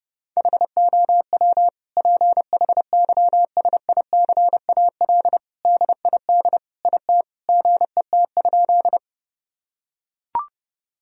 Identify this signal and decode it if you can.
Morse code, heard as a keyed tone, about 30 wpm, 700 Hz. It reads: HOW PHYSICAL DID IT GET?
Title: Sentences from Top 1000 Words - Review Part 4 - 30wpm